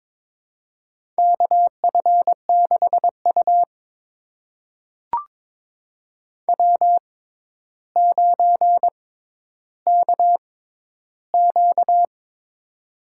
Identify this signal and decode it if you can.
Morse code, heard as a keyed tone, about 22 wpm, spaced as for 11 wpm, 700 Hz. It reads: KF6U W9KQ
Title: USA Call Signs - 4 Characters - 6x Character Spacing - 22wpm